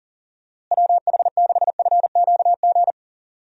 Morse code, heard as a keyed tone, about 40 wpm, 700 Hz. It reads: WH6FQG
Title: USA Call Signs - 6 Characters - Repeated 3x - 40wpm